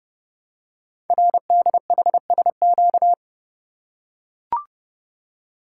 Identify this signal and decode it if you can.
Morse code, heard as a keyed tone, about 28 wpm, 700 Hz. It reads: RDHSQ
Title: Code Groups - 5 Letters - 30wpm